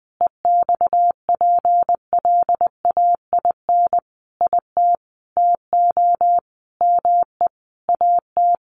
Morse code, heard as a keyed tone, about 20 wpm, 700 Hz. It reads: EXPLAIN IT TO ME AT
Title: Sentences from Top 1100 Words 20wpm